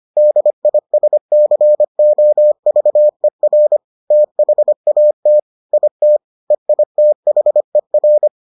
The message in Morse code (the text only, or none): DISCOVER THAT IT EITHER